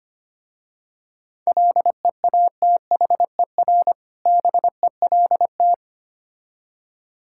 Morse code, heard as a keyed tone, about 25 wpm, 700 Hz. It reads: LEATHER BELT
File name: practice / 25wpm / Difficult Words with Hints - 25wpm